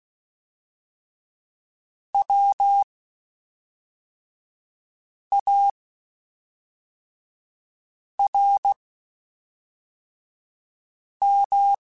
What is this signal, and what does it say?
Morse code, heard as a keyed tone, about 16 wpm, spaced as for 3 wpm, 800 Hz. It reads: WARM
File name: Spaced Characters / Spaced Characters-mp3s / Weather 4 10 sp @5